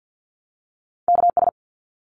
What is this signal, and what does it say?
Morse code, heard as a keyed tone, about 50 wpm, 700 Hz. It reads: KS
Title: ARRL Sections 50wpm